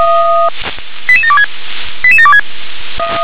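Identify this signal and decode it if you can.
Five-tone selective call (ZVEI), heard as quick runs of five tones, tones of about 70 ms. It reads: 88726 88726
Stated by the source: ZVEI five-tone call received by SDR.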